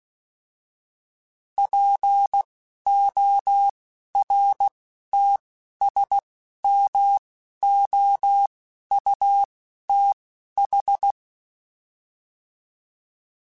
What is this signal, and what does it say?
Morse code, heard as a keyed tone, about 16 wpm, spaced as for 11 wpm, 800 Hz. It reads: PORTSMOUTH